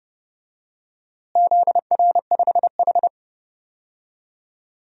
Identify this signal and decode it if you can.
Morse code, heard as a keyed tone, about 32 wpm, 700 Hz. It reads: ZR5H
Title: DXCC-100 Call Signs - 4 Characters - Repeated 2x - 30wpm